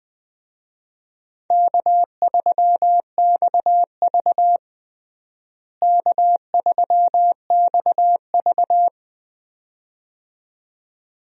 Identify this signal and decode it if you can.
Morse code, heard as a keyed tone, about 20 wpm, 700 Hz. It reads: K3XV K3XV
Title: USA Call Signs - 4 Characters - Repeated 2x - 20wpm